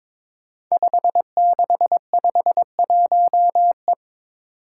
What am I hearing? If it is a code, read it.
Morse code, heard as a keyed tone, about 22 wpm, 700 Hz. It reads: 5651E
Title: Five Challenging Letter-Number Combinations 22wpm